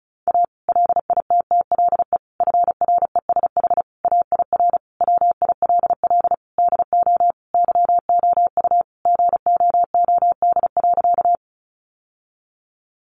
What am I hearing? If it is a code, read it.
Morse code, heard as a keyed tone, about 34 wpm, 700 Hz. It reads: A LITTLE FRESH AIR WILL DO YOU GOOD.
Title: Sentences from Top 1200 Words 35wpm